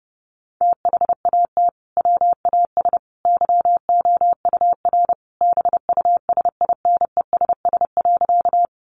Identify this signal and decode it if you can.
Morse code, heard as a keyed tone, about 30 wpm, 700 Hz. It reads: THAT WAS YOUR BUSINESS.
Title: Sentences from Top 600 Words 30wpm